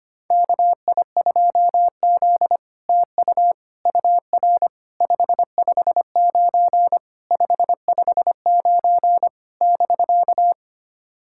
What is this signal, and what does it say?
Morse code, heard as a keyed tone, about 25 wpm, 700 Hz. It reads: KI2Z TU UR 559 559 <BK>
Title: SOTA QSO Practice 25wpm